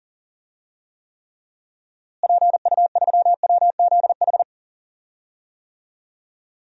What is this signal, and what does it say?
Morse code, heard as a keyed tone, about 40 wpm, 700 Hz. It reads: PU3WZH
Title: DXCC-100 Call Signs - 6 Characters - Repeated 2x - 40wpm